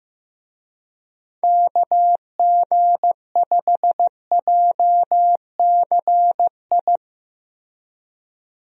Morse code, heard as a keyed tone, about 15 wpm, 700 Hz. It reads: KG5JCI
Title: USA Call Signs - 6 Characters - Repeated 3x - 15wpm